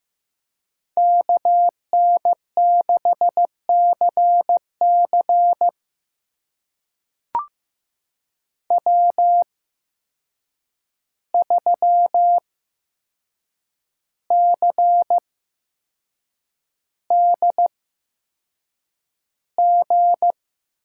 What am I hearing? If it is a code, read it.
Morse code, heard as a keyed tone, about 15 wpm, spaced as for 6 wpm, 700 Hz. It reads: KN6CC W3CDG
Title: USA Call Signs - 5 Characters - 8x Character Spacing - 15wpm